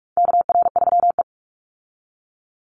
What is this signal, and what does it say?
Morse code, heard as a keyed tone, about 44 wpm, 700 Hz. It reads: KR3E
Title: Call Signs - 45wpm